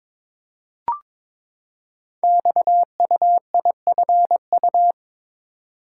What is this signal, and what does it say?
Morse code, heard as a keyed tone, about 22 wpm, 700 Hz. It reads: XUIFU
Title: Code Groups - 5 Letters - 22wpm